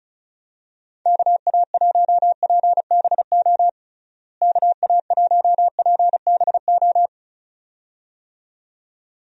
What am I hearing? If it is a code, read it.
Morse code, heard as a keyed tone, about 34 wpm, 700 Hz. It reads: KA1PBO KA1PBO
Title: USA Call Signs - 6 Characters - Repeated 2x - 35wpm